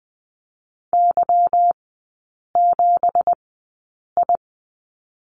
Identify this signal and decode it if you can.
Morse code, heard as a keyed tone, about 19 wpm, spaced as for 8 wpm, 690 Hz. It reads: Y7I
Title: 166 Lesson 27e - Number 7 - Three Characters 20wpm